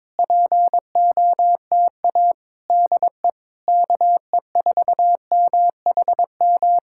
Morse code, heard as a keyed tone, about 22 wpm, 700 Hz. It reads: POTA DE KE4MHM
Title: Speed-Racing Common QSO Phrases - 22wpm